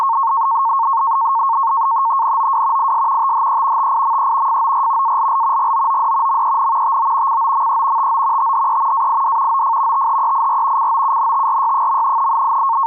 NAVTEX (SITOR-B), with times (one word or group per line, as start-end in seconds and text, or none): 2.2-2.9 NOW
2.8-3.3 IS
3.2-3.9 THE
3.8-4.6 TIME
4.5-5.2 FOR
5.0-5.7 ALL
5.6-6.4 GOOD
6.3-7.0 MEN
6.8-7.4 TO
7.3-8.1 COME
8.0-8.5 TO
8.4-9.1 THE
8.9-9.6 AID
9.5-10.1 OF
9.9-10.9 THEIR
10.8-12.3 COUNTRY.